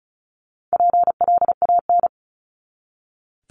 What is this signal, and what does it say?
Morse code, heard as a keyed tone, about 34 wpm, 700 Hz. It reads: PLAN
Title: ING Suffix - Top 100 Words with Base Words 35wpm